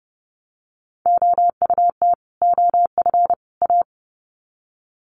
Morse code, heard as a keyed tone, about 30 wpm, 700 Hz. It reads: OUT OF A
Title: Sets of 3 Words 30wpm